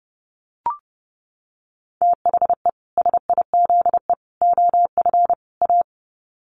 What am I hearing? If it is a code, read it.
Morse code, heard as a keyed tone, about 30 wpm, 700 Hz. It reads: THE SIZE OF A
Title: Sets of 4 Words 30wpm